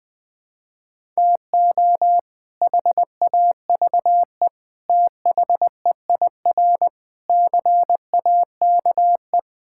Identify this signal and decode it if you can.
Morse code, heard as a keyed tone, about 21 wpm, 700 Hz. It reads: TO HAVE THEIR CAKE
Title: Sets of 8 Words - 20wpm